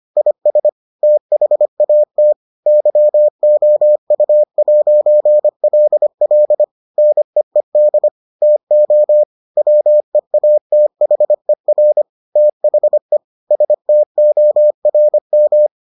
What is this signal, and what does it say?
Morse code, heard as a keyed tone, about 25 wpm, 600 Hz. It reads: IS THAT YOU'LL NEED TO WEATHER THE STORM